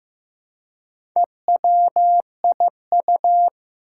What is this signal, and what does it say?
Morse code, heard as a keyed tone, about 15 wpm, 700 Hz. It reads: EWIU
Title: Four Challenging Letter-Number Combinations 15wpm